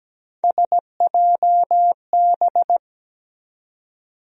Morse code, heard as a keyed tone, about 17 wpm, 700 Hz. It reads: SJB
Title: USA Call Sign Postfixes - 3 Letters 17wpm